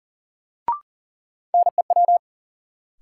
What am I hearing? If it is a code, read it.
Morse code, heard as a keyed tone, about 40 wpm, 700 Hz. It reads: NEW
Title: Sentences from Top 300 Words - New Word Drill - 40wpm